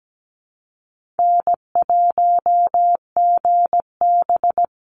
Morse code, heard as a keyed tone, about 17 wpm, 700 Hz. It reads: N1GB
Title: NA SOTA Chaser Callsigns 17wpm